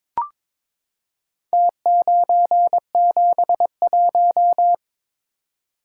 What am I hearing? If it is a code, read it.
Morse code, heard as a keyed tone, about 22 wpm, 700 Hz. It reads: T971
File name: practice / 22wpm / Four Letter-Number Combinations - 22wpm